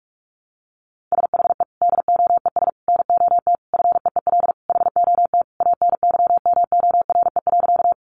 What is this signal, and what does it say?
Morse code, heard as a keyed tone, about 44 wpm, 700 Hz. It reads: SHE DOES NOT FEEL HOT ANYMORE.
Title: Easy Phrases 45wpm